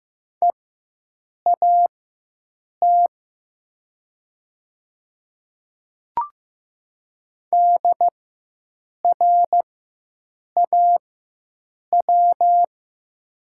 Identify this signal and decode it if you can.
Morse code, heard as a keyed tone, about 15 wpm, spaced as for 7 wpm, 700 Hz. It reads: EAT DRAW
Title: 4x Character Spacing - Top 300 Words - 15wpm